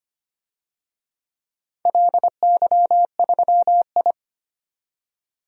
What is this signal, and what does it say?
Morse code, heard as a keyed tone, about 25 wpm, 700 Hz. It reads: LY3I
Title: DXCC-100 Call Signs - 4 Characters - 25wpm